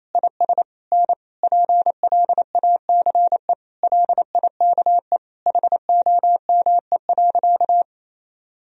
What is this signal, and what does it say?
Morse code, heard as a keyed tone, about 28 wpm, 700 Hz. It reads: IS N PLACE LIKE HOME.
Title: Sentences from Top 200 Words and Abbreviations 28wpm